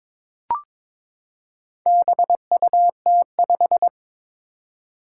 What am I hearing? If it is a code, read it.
Morse code, heard as a keyed tone, about 22 wpm, 700 Hz. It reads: BUT5
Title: Four Challenging Letter-Number Combinations 22wpm